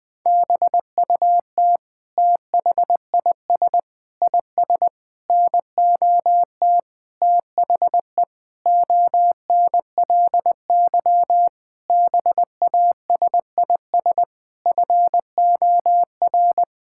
Morse code, heard as a keyed tone, about 21 wpm, 700 Hz. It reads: BUT THIS IS NOT THE ONLY BASIS FOR
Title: Sentences from Top 1600 Words 20wpm